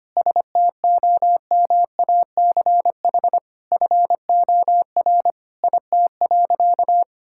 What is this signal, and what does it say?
Morse code, heard as a keyed tone, about 25 wpm, 700 Hz. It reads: STOMACH FOR IT.
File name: practice / 25wpm / All-Age Appropriate Jokes - 25wpm